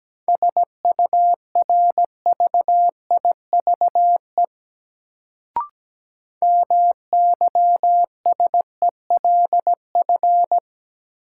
Morse code, heard as a keyed tone, about 17 wpm, 700 Hz. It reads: SURVIVE MYSELF
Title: Sentences from Top 1300 Words - New Word Drill - 17wpm